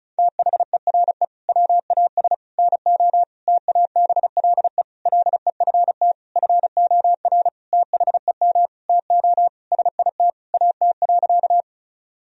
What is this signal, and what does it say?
Morse code, heard as a keyed tone, about 36 wpm, 700 Hz. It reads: THERE WAS NO TABLE LEFT FOR THEM TO SIT AT.
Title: Sentences from Top 1000 Words - Review Part 2 - 35wpm